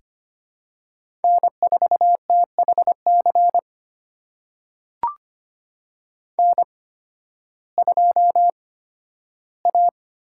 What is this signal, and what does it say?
Morse code, heard as a keyed tone, about 25 wpm, spaced as for 13 wpm, 700 Hz. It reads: N4THC N2A
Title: USA Call Signs - 5 Characters - 8x Character Spacing - 25wpm